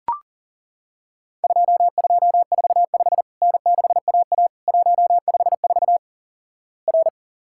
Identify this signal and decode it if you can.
Morse code, heard as a keyed tone, about 38 wpm, 700 Hz. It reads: 2245 N6AA 154 R
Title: Worked All Europe DX Contest - QTC Exchange - 40wpm